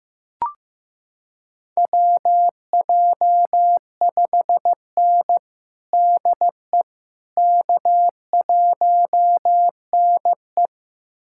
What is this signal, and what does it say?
Morse code, heard as a keyed tone, about 15 wpm, 700 Hz. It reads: WJ5N DE K1NE
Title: USA Callsign DE Callsign - 4 Characters - 15wpm